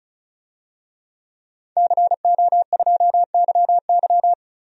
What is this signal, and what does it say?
Morse code, heard as a keyed tone, about 36 wpm, 700 Hz. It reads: CO2YY
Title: DXCC-100 Call Signs - 5 Characters - Repeated 3x - 35wpm